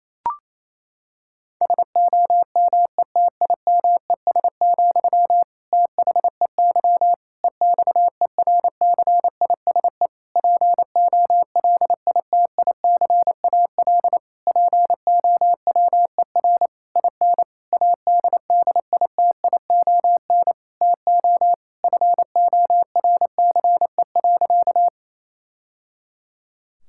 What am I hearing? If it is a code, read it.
Morse code, heard as a keyed tone, about 28 wpm, 700 Hz. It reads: SOMETIMES, THEY EXERCISE POLITICAL POWER IN ADDITION TO FORCE.
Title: Sentences from Top 1000 Words - Review Part 6 - 28wpm